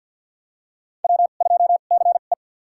Morse code, heard as a keyed tone, about 52 wpm, spaced as for 37 wpm, 700 Hz. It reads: W2CE
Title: USA Call Signs - 2x Character Spacing - 50wpm